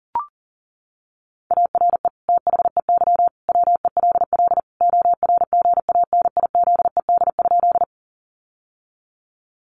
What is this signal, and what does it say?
Morse code, heard as a keyed tone, about 37 wpm, 700 Hz. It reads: ARE THEY WELL ORGANIZED?
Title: Sentences from Top 1300 Words 40wpm